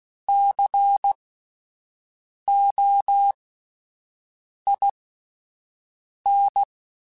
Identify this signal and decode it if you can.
Morse code, heard as a keyed tone, about 16 wpm, spaced as for 5 wpm, 800 Hz. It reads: COIN
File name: Spaced Alphas / Spaced Alphas mp3s / Cs sp @10